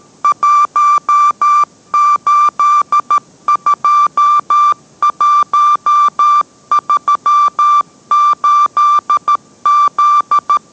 Morse code, heard as a keyed tone, about 13 wpm, 1180 Hz. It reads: 182138Z